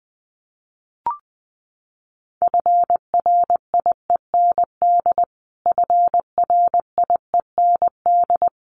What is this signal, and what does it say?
Morse code, heard as a keyed tone, about 20 wpm, 700 Hz. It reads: FRIEND FRIEND
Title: Top 300 Words Repeated 20wpm